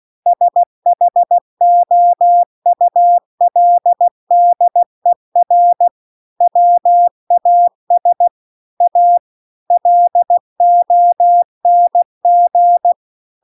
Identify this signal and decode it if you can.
Morse code, heard as a keyed tone, about 16 wpm, 700 Hz. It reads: SHOULDER WAS A LONG